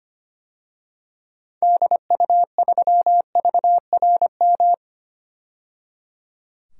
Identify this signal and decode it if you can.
Morse code, heard as a keyed tone, about 25 wpm, 700 Hz. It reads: DU3VRM
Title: DXCC-100 Call Signs - 6 Characters - 25wpm